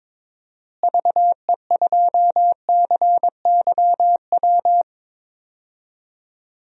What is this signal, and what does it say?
Morse code, heard as a keyed tone, about 22 wpm, 700 Hz. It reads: VE2CYW
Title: DXCC-100 Call Signs - 6 Characters - Repeated 3x - 22wpm